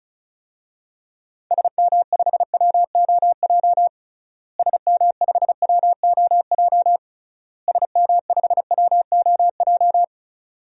Morse code, heard as a keyed tone, about 34 wpm, 700 Hz. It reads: SM5WOJ SM5WOJ SM5WOJ
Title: DXCC-100 Call Signs - 6 Characters - Repeated 3x - 35wpm